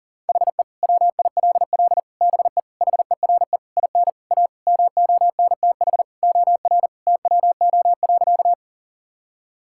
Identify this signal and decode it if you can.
Morse code, heard as a keyed tone, about 40 wpm, 700 Hz. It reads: HE WILL BE HERE IN A MONTH OR TWO.
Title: Sentences from Top 1000 Words - Review Part 5 - 40wpm